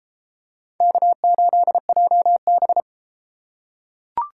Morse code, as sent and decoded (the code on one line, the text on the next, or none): -.- ---.. .--- -...
K8JB